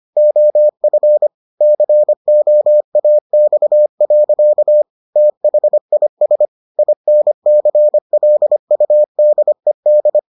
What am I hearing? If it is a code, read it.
Morse code, heard as a keyed tone, about 25 wpm, 600 Hz. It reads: OF COAX. THIS INCLUDED